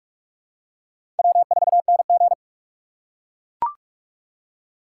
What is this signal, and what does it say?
Morse code, heard as a keyed tone, about 42 wpm, 700 Hz. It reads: W4NG